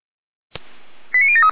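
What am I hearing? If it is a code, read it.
Five-tone selective call (ZVEI), heard as quick runs of five tones, tones of about 70 ms. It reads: 89073
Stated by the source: ZVEI five-tone call 89011 received by SDR.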